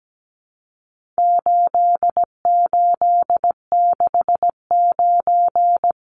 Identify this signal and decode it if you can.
Morse code, heard as a keyed tone, about 17 wpm, 700 Hz. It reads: 8869
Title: Numbers - 4-Digit 17wpm